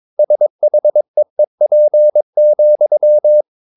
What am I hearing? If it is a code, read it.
Morse code, heard as a keyed tone, about 22 wpm, 600 Hz. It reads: SHEEP,
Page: Romanen Never af Ken Follet på engelsk.